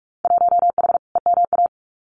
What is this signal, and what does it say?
Morse code, heard as a keyed tone, about 46 wpm, 700 Hz. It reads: 1H EMA
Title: Winter Field Day Exchanges 45wpm